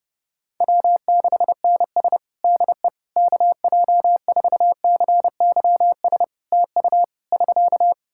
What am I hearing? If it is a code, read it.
Morse code, heard as a keyed tone, about 30 wpm, 700 Hz. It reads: W6NS DE KJ4CYS TU <SK>
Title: Practice QSO - Exchange 4 - Ending - 30wpm